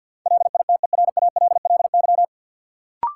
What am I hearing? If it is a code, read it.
Morse code, heard as a keyed tone, about 51 wpm, 700 Hz. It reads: LITERALLY
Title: Sentences from Top 1400 Words - New Word Drill - 50wpm